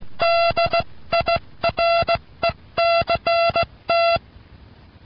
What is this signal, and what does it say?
Morse code, heard as a keyed tone, about 17 wpm, spaced as for 15 wpm, 680 Hz, with noise. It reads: DIRECT